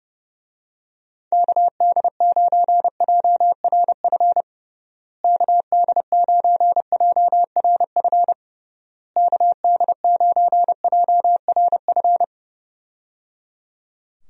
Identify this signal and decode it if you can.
Morse code, heard as a keyed tone, about 30 wpm, 700 Hz. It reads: KD9JRF KD9JRF KD9JRF
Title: USA Call Signs - 6 Characters - Repeated 3x - 30wpm